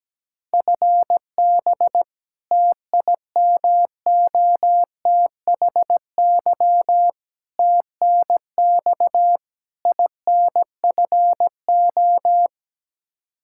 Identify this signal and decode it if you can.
Morse code, heard as a keyed tone, about 17 wpm, 700 Hz. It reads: FB TIMOTHY TNX INFO
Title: Practice QSO - Exchange 3 - More Info - 17wpm